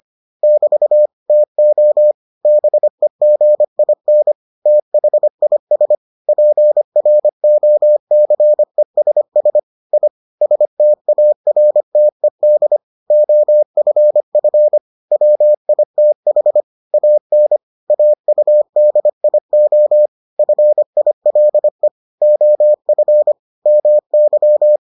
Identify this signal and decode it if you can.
Morse code, heard as a keyed tone, about 25 wpm, 600 Hz. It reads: <BT> TO BEGIN THIS PROCESS I STARTED OFF WITH AN AUDIO FILE OF MY